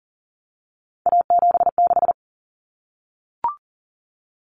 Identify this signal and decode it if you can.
Morse code, heard as a keyed tone, about 40 wpm, 700 Hz. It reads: A76
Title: Common Cut Numbers - 3-Digit 40wpm